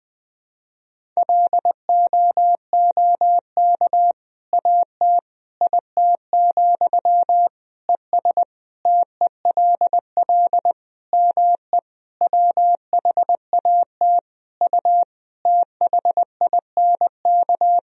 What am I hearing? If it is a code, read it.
Morse code, heard as a keyed tone, about 20 wpm, 700 Hz. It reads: LOOK AT IT, ES TELL ME WHAT U THINK
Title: Sentences from Top 200 Words and Abbreviations 20wpm